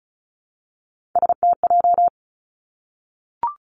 Morse code, heard as a keyed tone, about 36 wpm, 700 Hz. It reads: STJ
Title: Three Challenging Letter-Number Combinations 35wpm